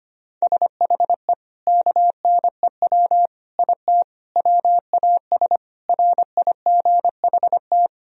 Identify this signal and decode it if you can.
Morse code, heard as a keyed tone, about 25 wpm, 700 Hz. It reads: SHE KNEW IT WAS RIGHT
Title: Sentences from Top 1000 Words - Review Part 2 - 25wpm